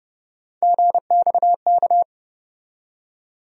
Morse code, heard as a keyed tone, about 30 wpm, 700 Hz. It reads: GXK
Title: USA Call Sign Postfixes - 3 Letters 30wpm